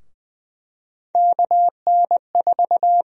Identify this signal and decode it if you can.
Morse code, heard as a keyed tone, about 20 wpm, 700 Hz. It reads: KN4
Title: USA Call Sign Prefixes - 20wpm